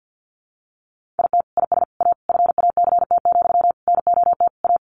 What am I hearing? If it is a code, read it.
Morse code, heard as a keyed tone, about 50 wpm, 700 Hz. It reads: IT IS A FACT, NOT A
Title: Sentences from Top 1400 Words 50wpm